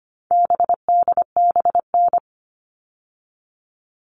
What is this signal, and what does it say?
Morse code, heard as a keyed tone, about 25 wpm, 700 Hz. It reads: BDBN
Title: All Cut Numbers - 4-Digit 25wpm